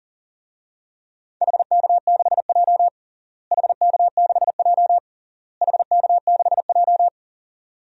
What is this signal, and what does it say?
Morse code, heard as a keyed tone, about 40 wpm, 700 Hz. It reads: HK6J HK6J HK6J
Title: DXCC-100 Call Signs - 4 Characters - Repeated 3x - 40wpm